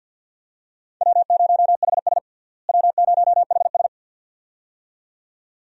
Morse code, heard as a keyed tone, about 47 wpm, 690 Hz. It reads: W0HS W0HS
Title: USA Call Signs - 4 Characters - Repeated 2x - 50wpm